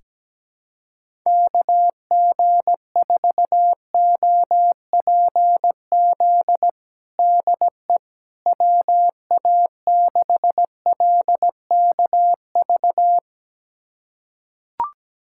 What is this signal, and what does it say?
Morse code, heard as a keyed tone, about 17 wpm, 700 Hz. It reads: KG4OPZ DE WA6LKV
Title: USA Callsign DE Callsign - 6 Characters - 17wpm